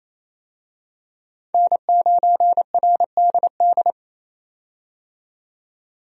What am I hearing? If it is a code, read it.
Morse code, heard as a keyed tone, about 28 wpm, 700 Hz. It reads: N9RDD
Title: USA Call Signs - 5 Characters - 28wpm